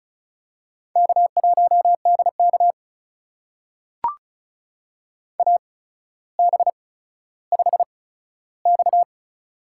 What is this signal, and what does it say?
Morse code, heard as a keyed tone, about 36 wpm, spaced as for 15 wpm, 700 Hz. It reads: K1DK AB5X
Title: USA Call Signs - 4 Characters - 8x Character Spacing - 35wpm